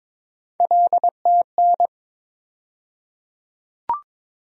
Morse code, heard as a keyed tone, about 22 wpm, 700 Hz. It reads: LTN
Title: USA Call Sign Postfixes - 3 Letters 22wpm